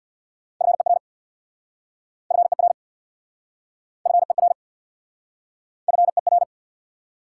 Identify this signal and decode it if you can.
Morse code, heard as a keyed tone, about 69 wpm, 700 Hz. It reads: FER FER FER FER